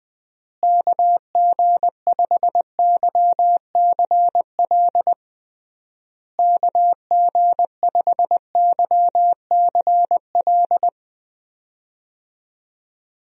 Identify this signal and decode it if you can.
Morse code, heard as a keyed tone, about 20 wpm, 700 Hz. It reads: KG5YCL KG5YCL
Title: USA Call Signs - 6 Characters - Repeated 3x - 20wpm